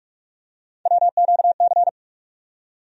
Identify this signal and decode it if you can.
Morse code, heard as a keyed tone, about 45 wpm, 700 Hz. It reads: WQC
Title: USA Call Sign Postfixes - 3 Letters 45wpm